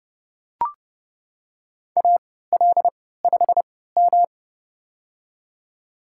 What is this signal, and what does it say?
Morse code, heard as a keyed tone, about 30 wpm, spaced as for 17 wpm, 700 Hz. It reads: AL5M
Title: USA Call Signs - 4 Characters - 3x Character Spacing - 30wpm